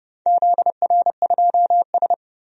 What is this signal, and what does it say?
Morse code, heard as a keyed tone, about 29 wpm, 700 Hz. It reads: ZR2S
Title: DXCC-100 Call Signs - 4 Characters - 30wpm